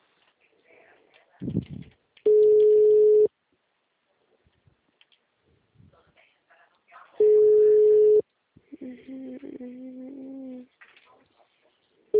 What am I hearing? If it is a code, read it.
Ringback tone (Europe, 425 Hz), with speech.